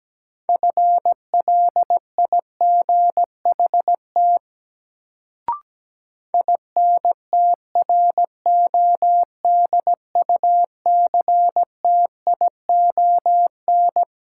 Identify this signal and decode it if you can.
Morse code, heard as a keyed tone, about 17 wpm, 700 Hz. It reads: FLIGHT INTRODUCTION
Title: Sentences from Top 1000 Words - New Word Drill - 17wpm